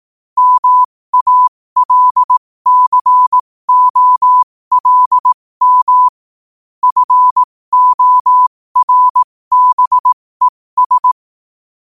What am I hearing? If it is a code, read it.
Morse code, heard as a keyed tone, about 18 wpm, spaced as for 15 wpm, 1000 Hz. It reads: MALCOLM FORBES
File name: qotdfiles / 15wpm